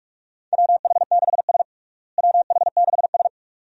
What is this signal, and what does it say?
Morse code, heard as a keyed tone, about 42 wpm, 700 Hz. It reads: WH6S WH6S
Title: USA Call Signs - 4 Characters - Repeated 2x - 45wpm